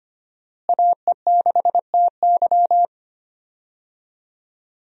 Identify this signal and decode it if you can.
Morse code, heard as a keyed tone, about 25 wpm, 700 Hz. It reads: AE6TY
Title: USA Call Signs - 5 Characters - Repeated 2x - 25wpm